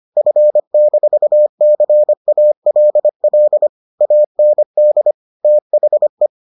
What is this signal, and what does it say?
Morse code, heard as a keyed tone, about 25 wpm, 600 Hz. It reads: F-CALL AND THE